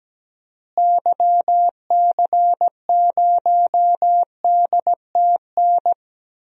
Morse code, heard as a keyed tone, about 17 wpm, 700 Hz. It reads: YC0DTN
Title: DXCC-100 Call Signs - 6 Characters - Repeated 2x - 17wpm